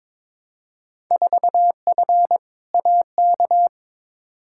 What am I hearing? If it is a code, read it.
Morse code, heard as a keyed tone, about 22 wpm, 700 Hz. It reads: 4F AK
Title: Field Day Exchanges Encore - 22wpm